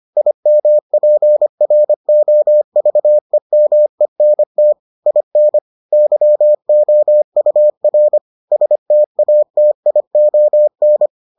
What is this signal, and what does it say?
Morse code, heard as a keyed tone, about 25 wpm, 600 Hz. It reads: IMPROVEMENT IN YOUR STATION